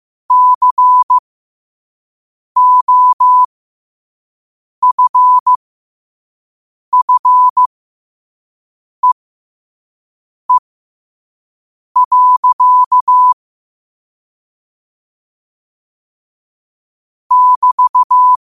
Morse code, heard as a keyed tone, about 15 wpm, spaced as for 5 wpm, 1000 Hz. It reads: COFFEE. <BT>
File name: qotdfiles / 5wpm